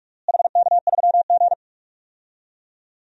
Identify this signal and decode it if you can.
Morse code, heard as a keyed tone, about 46 wpm, 700 Hz. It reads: HK3G
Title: DXCC-100 Call Signs - 4 Characters - Repeated 3x - 45wpm